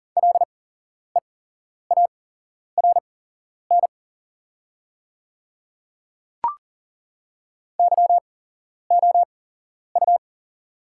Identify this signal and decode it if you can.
Morse code, heard as a keyed tone, about 40 wpm, spaced as for 11 wpm, 700 Hz. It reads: LEARN YOU
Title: 8x Character Spacing - Top 300 Words - 40wpm